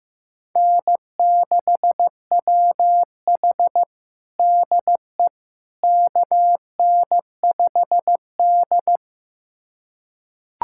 Morse code, heard as a keyed tone, about 16 wpm, 700 Hz. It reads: N6WH DE KN5D
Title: USA Callsign DE Callsign - 4 Characters - 15wpm